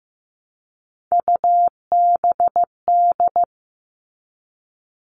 Morse code, heard as a keyed tone, about 15 wpm, 700 Hz. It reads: UBD
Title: All Cut Numbers - 3-Digit 15wpm